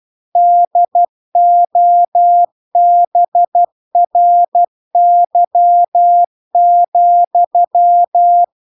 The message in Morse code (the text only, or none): DOBRY,